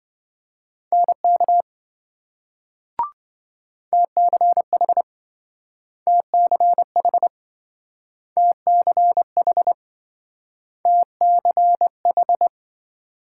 Morse code, heard as a keyed tone, about 27 wpm, 700 Hz. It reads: NK TCH TCH TCH TCH